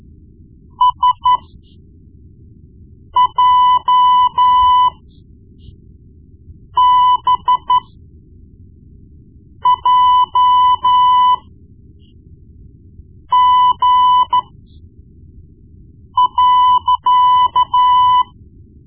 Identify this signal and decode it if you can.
Morse code, heard as a keyed tone, about 12 wpm, spaced as for 4 wpm, 1000 Hz. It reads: SJBJG.
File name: Morse Lessons / Morse Lessons / Tape-1B